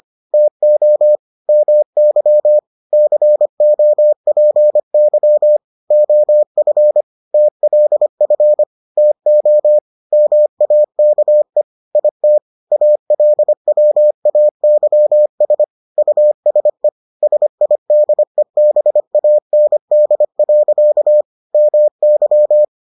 Morse code, heard as a keyed tone, about 25 wpm, 600 Hz. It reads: TO MY COPY OF TLF TO MAKE IT ALWAYS USE SIDEBAND. MY